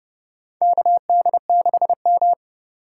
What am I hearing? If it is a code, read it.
Morse code, heard as a keyed tone, about 31 wpm, 700 Hz. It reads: KD6M
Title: USA Call Signs - 4 Characters - 3x Character Spacing - 30wpm